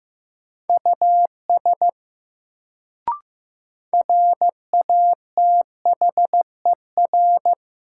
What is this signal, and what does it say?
Morse code, heard as a keyed tone, about 15 wpm, 700 Hz. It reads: US RATHER
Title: Sets of 2 Words - New Word Drill - 15wpm